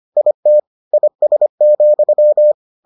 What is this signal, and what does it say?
Morse code, heard as a keyed tone, about 25 wpm, 600 Hz. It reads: IT IS,